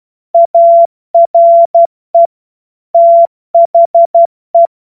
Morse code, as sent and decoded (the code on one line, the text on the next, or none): .- .-. / - .... .
ARE THE